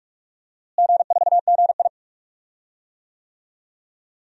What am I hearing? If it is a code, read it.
Morse code, heard as a keyed tone, about 45 wpm, 700 Hz. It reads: G4GI
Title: DXCC-100 Call Signs - 4 Characters - Repeated 3x - 45wpm